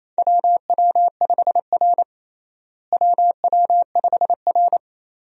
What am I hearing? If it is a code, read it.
Morse code, heard as a keyed tone, about 28 wpm, 700 Hz. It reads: WW5R WW5R
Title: USA Call Signs - 4 Characters - Repeated 2x - 28wpm